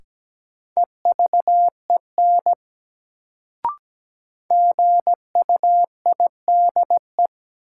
Morse code, heard as a keyed tone, about 17 wpm, 700 Hz. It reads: EVEN GUIDE
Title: Sentences from Top 600 Words - New Word Drill - 17wpm